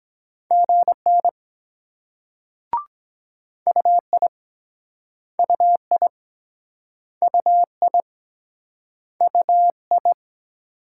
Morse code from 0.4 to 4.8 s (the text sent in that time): GN UI